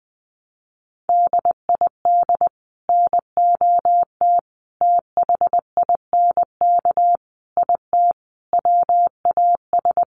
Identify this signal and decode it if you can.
Morse code, heard as a keyed tone, about 20 wpm, 700 Hz. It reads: DID NOT THINK IT WAS
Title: Sets of 5 Words Encore 20wpm